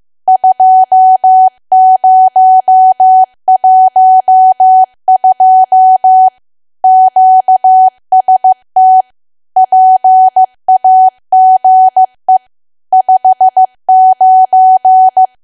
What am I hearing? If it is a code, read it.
Morse code, heard as a keyed tone, about 15 wpm, 750 Hz. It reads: 2012 QST PAGE 59
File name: cwa / W1AW-Level2 / 15WPM